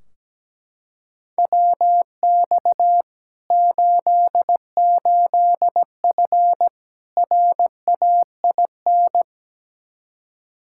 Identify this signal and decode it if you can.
Morse code, heard as a keyed tone, about 17 wpm, 700 Hz. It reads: WX 88F RAIN
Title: Practice QSO - Exchange 2 - Info - 17wpm